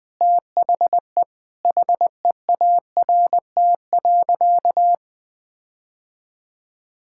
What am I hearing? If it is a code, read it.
Morse code, heard as a keyed tone, about 20 wpm, 700 Hz. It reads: THE HEART.
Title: Sentences from Top 1600 Words 20wpm